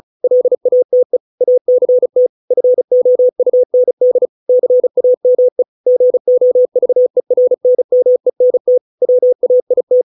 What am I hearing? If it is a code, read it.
Morse code, heard as a keyed tone, about 34 wpm, 500 Hz. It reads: LATE ACT FOUND CAME GOVERNMENT WAIT